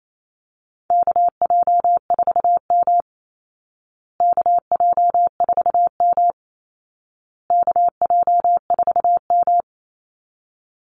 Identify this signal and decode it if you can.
Morse code, heard as a keyed tone, about 28 wpm, 700 Hz. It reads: KJ4M KJ4M KJ4M
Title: Call Signs Repeated 2x - 28wpm